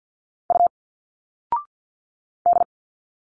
Morse code, heard as a keyed tone, about 52 wpm, 700 Hz. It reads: U D
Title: Single Letters 50wpm